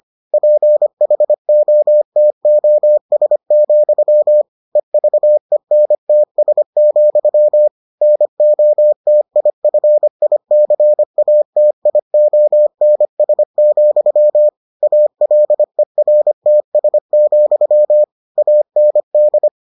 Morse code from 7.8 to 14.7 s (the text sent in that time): NOTIFICATIONS,